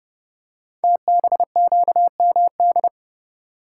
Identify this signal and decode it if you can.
Morse code, heard as a keyed tone, about 30 wpm, 700 Hz. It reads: TBQMD
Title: Code Groups - 5 Letters - 30wpm